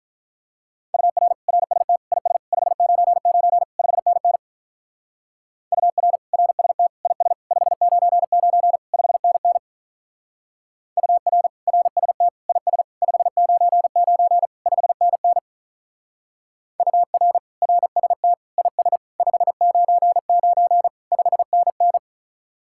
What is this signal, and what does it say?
Morse code, heard as a keyed tone, about 43 wpm, 700 Hz. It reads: UR RST IS 599 5NN UR RST IS 599 5NN UR RST IS 599 5NN UR RST IS 599 5NN
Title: Speed-Racing Common QSO Phrases - 35wpm